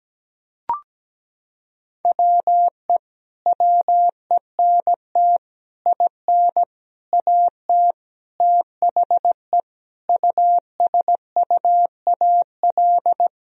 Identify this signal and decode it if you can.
Morse code, heard as a keyed tone, about 17 wpm, 700 Hz. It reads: WE WENT IN AT THE USUAL
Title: Sentences from Top 1000 Words - Review Part 2 - 17wpm